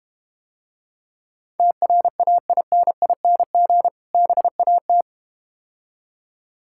Morse code, heard as a keyed tone, about 32 wpm, 700 Hz. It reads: TRAINING BAT